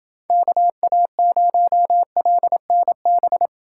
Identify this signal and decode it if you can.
Morse code, heard as a keyed tone, about 29 wpm, 700 Hz. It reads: KA0LNB